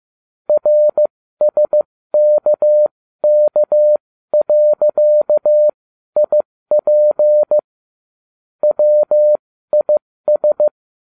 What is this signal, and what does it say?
Morse code, heard as a keyed tone, about 15 wpm, spaced as for 13 wpm, 600 Hz. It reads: RSKK.IP WIS